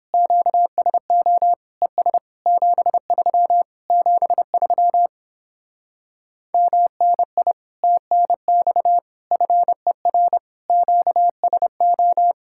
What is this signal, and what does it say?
Morse code, heard as a keyed tone, about 29 wpm, 700 Hz. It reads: QSO ES 73 73 MNI TNX FER QSO